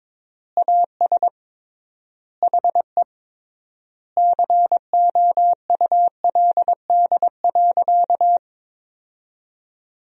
Morse code, heard as a keyed tone, about 22 wpm, 700 Hz. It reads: AS HE COULD.
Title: 3x Word-Spacing - Sentences from Top 500 Words - 22wpm